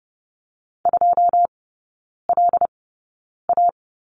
Morse code, heard as a keyed tone, about 30 wpm, spaced as for 9 wpm, 700 Hz. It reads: 2LA
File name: Lesson 16e - Letter c - Three Characters 30wpm